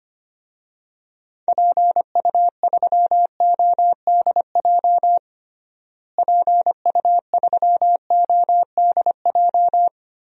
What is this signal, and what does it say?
Morse code, heard as a keyed tone, about 25 wpm, 700 Hz. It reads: PU3ODJ PU3ODJ
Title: DXCC-100 Call Signs - 6 Characters - Repeated 3x - 25wpm